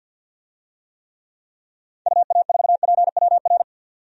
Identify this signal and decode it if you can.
Morse code, heard as a keyed tone, about 46 wpm, 700 Hz. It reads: UA4PWR
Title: DXCC-100 Call Signs - 6 Characters - 50wpm